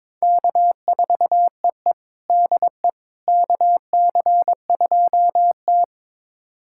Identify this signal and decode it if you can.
Morse code, heard as a keyed tone, about 21 wpm, 700 Hz. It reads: K4EE DE KC2T
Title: USA Callsign DE Callsign - 4 Characters - 22wpm